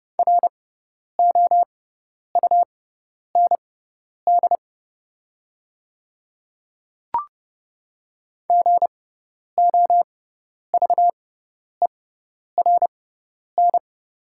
Morse code, heard as a keyed tone, about 30 wpm, spaced as for 10 wpm, 700 Hz. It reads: ROUND GOVERN
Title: 6x Character Spacing - Top 500 Words - 30wpm